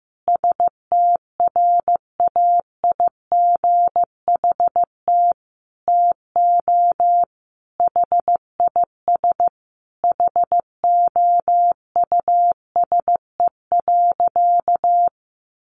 Morse code, heard as a keyed tone, about 15 wpm, 700 Hz. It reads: STRAIGHT TO HIS HOUSE.
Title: Sentences from Top 1000 Words 15wpm